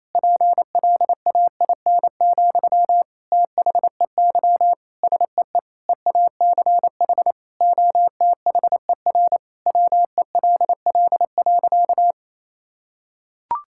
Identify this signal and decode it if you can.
Morse code, heard as a keyed tone, about 28 wpm, 700 Hz. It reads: PLAIN, THEY SEE EACH OTHER WELL.
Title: Sentences from Top 1000 Words - Review Part 5 - 28wpm